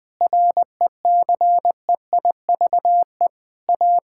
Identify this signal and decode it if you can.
Morse code, heard as a keyed tone, about 20 wpm, 700 Hz. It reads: RECEIVE A
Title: Sets of 8 Words - 20wpm